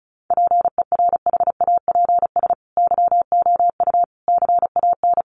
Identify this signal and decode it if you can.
Morse code, heard as a keyed tone, about 34 wpm, 700 Hz. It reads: PERHAPS YOU CAN
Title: Sentences from Top 1200 Words 35wpm